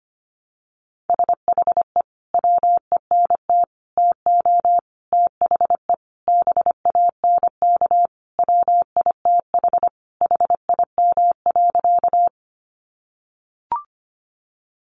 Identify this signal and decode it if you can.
Morse code, heard as a keyed tone, about 25 wpm, 700 Hz. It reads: SHE WENT TO THE BANK WITH HIM.
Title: Sentences from Top 900 Words 25wpm